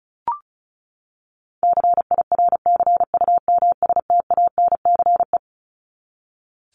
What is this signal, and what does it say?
Morse code, heard as a keyed tone, about 34 wpm, 700 Hz. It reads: CIRCUMSTANCE
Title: ES:S Suffix - Top 100 Words with Base Words 35wpm